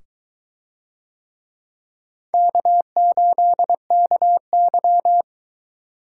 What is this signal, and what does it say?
Morse code, heard as a keyed tone, about 22 wpm, 700 Hz. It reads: K8KY